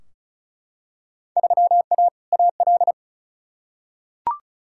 Morse code, heard as a keyed tone, about 36 wpm, 700 Hz. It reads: 3A AL
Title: Field Day Exchanges - 35wpm